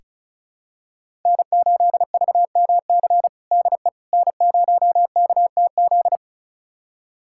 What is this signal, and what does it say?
Morse code, heard as a keyed tone, about 34 wpm, 700 Hz. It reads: N8VMC DE N0KTZ
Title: USA Callsign DE Callsign - 5 Characters - 35wpm